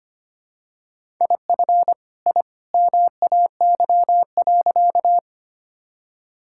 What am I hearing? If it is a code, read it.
Morse code, heard as a keyed tone, about 25 wpm, 700 Hz. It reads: IF I MAY.
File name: practice / 25wpm / Sentences with 2-Word Phrases - 25wpm